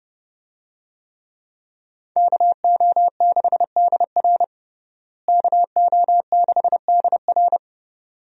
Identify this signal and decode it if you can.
Morse code, heard as a keyed tone, about 30 wpm, 700 Hz. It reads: KO6DR KO6DR
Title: USA Call Signs - 5 Characters - Repeated 2x - 30wpm